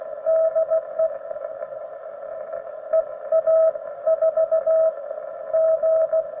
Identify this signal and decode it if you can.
Morse code, heard as a keyed tone, about 16 wpm, spaced as for 13 wpm, 640 Hz. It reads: DE EA4G